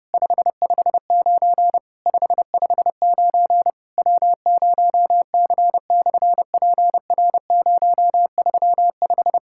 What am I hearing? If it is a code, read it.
Morse code, heard as a keyed tone, about 30 wpm, 700 Hz. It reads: 559 559 W0C/PR035
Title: SOTA QSO Practice 30wpm